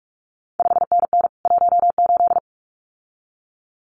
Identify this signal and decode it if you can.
Morse code, heard as a keyed tone, about 46 wpm, 700 Hz. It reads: 5NN 18
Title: CQ WW DX Contest Exchange Encore - 45wpm